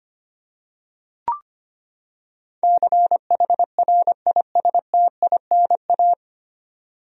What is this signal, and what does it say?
Morse code, heard as a keyed tone, about 25 wpm, 700 Hz. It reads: CHRISTINA
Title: USA Common Names - 25wpm